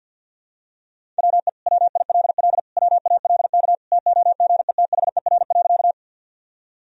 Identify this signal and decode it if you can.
Morse code, heard as a keyed tone, about 51 wpm, 700 Hz. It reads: WE WILL WALK TOGETHER.
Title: Sentences from Top 1000 Words - Review Part 1 - 50wpm